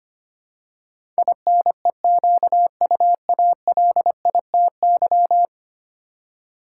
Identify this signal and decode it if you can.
Morse code, heard as a keyed tone, about 25 wpm, 700 Hz. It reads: INEQUALITY
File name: 10 Letter Words - First 3 Letters Given - 25wpm